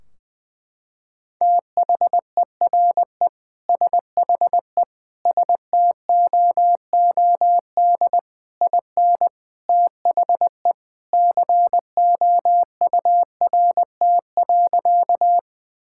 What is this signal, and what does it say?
Morse code, heard as a keyed tone, about 20 wpm, 700 Hz. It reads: THERE SHE STOOD IN THE COURT.
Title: Sentences from Top 1600 Words 20wpm